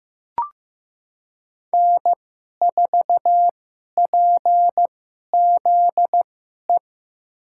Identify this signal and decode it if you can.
Morse code, heard as a keyed tone, about 15 wpm, spaced as for 11 wpm, 700 Hz. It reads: N4PZE
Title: USA Call Signs - 2x Character Spacing - 15wpm